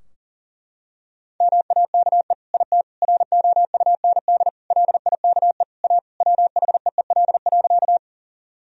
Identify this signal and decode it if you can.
Morse code, heard as a keyed tone, about 40 wpm, 700 Hz. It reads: MAKE IT ROUND LIKE A WHEEL.
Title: Sentences from Top 1000 Words - Review Part 3 - 40wpm